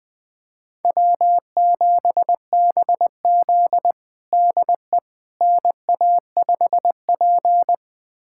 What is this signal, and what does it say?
Morse code, heard as a keyed tone, about 20 wpm, 700 Hz. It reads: W7BZ DE NA5P
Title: USA Callsign DE Callsign - 4 Characters - 20wpm